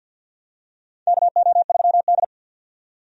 Morse code, heard as a keyed tone, about 47 wpm, 700 Hz. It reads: KO3D
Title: USA Call Signs - 4 Characters - 50wpm